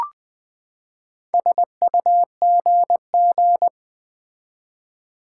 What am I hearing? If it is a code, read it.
Morse code, heard as a keyed tone, about 20 wpm, 700 Hz. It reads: SUGG
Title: Build A Word - 4 Letters At A Time - 10-Letter Words - 20wpm